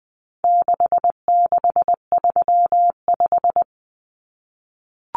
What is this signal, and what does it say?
Morse code, heard as a keyed tone, about 20 wpm, 700 Hz. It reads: 6635
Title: Numbers - 4-Digit 20wpm